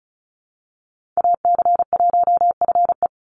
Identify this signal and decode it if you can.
Morse code, heard as a keyed tone, about 34 wpm, 700 Hz. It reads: AC1FE
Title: Call Signs - 35wpm